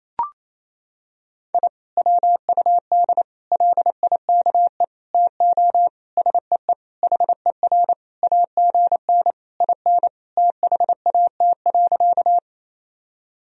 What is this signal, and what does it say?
Morse code, heard as a keyed tone, about 27 wpm, 700 Hz. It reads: I WUD LIKE TO SEE HER AGN IN THAT.
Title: Sentences from Top 200 Words and Abbreviations 28wpm